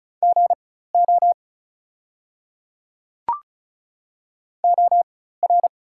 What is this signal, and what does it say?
Morse code, heard as a keyed tone, about 36 wpm, spaced as for 16 wpm, 700 Hz. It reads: GO OR
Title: 4x Character Spacing - Top 200 Words - 35wpm